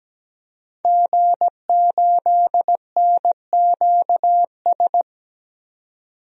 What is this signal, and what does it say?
Morse code, heard as a keyed tone, about 17 wpm, 700 Hz. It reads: G8NQS
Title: DXCC-100 Call Signs - 5 Characters - 17wpm